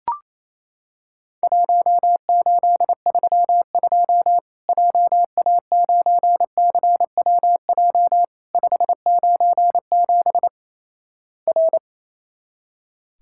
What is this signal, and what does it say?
Morse code, heard as a keyed tone, about 28 wpm, 700 Hz. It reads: 1832 JA9CWJ 597 R
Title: Worked All Europe DX Contest - QTC Exchange - 28wpm